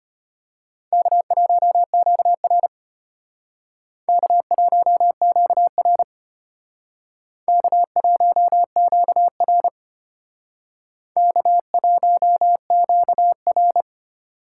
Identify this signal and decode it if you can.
Morse code, heard as a keyed tone, about 32 wpm, 700 Hz. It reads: K1QR K1QR K1QR K1QR